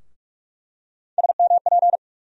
Morse code, heard as a keyed tone, about 41 wpm, 700 Hz. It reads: SMP
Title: USA Call Sign Postfixes - 3 Letters 45wpm